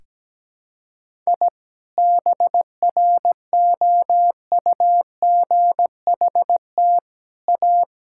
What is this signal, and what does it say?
Morse code, heard as a keyed tone, about 17 wpm, 700 Hz. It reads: I BROUGHT A
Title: Sentences from Top 1000 Words - Review Part 5 - 17wpm